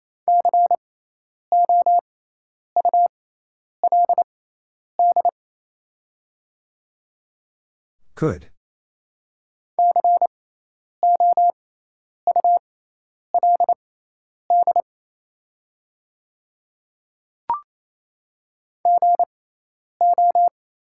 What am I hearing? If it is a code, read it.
Morse code, heard as a keyed tone, about 28 wpm, spaced as for 10 wpm, 700 Hz. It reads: COULD COULD GO
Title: 6x Character Spacing - Top 100 Words - 28wpm